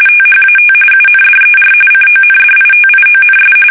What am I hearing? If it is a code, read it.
RTTY, about 45.45 baud. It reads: THAT UNDER CONDITIONS
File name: fsk850